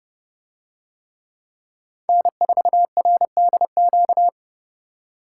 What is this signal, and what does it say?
Morse code, heard as a keyed tone, about 29 wpm, 700 Hz. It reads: N4RDQ